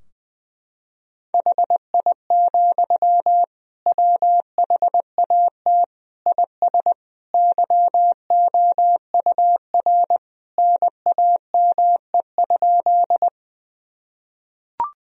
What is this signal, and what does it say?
Morse code, heard as a keyed tone, about 20 wpm, 700 Hz. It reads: HI, WHAT IS YOUR NAME?
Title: Sentences from Top 1000 Words - Review Part 3 - 20wpm